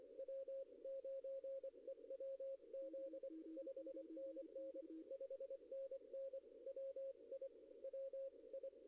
Morse code, heard as a keyed tone, about 24 wpm, 540 Hz. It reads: W9EWZ 5NN 5NN WI WI
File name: contests / arrl-160-k5tr-op-w5tn -0000